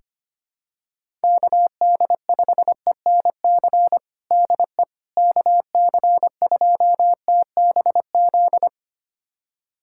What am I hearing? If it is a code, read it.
Morse code, heard as a keyed tone, about 25 wpm, 700 Hz. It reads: KD5ENC DE KC2TBZ
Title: USA Callsign DE Callsign - 6 Characters - 25wpm